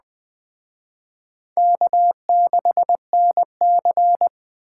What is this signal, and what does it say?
Morse code, heard as a keyed tone, about 20 wpm, 700 Hz. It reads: K6NC
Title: Speed Racing - USA 1x2 Call Signs 17wpm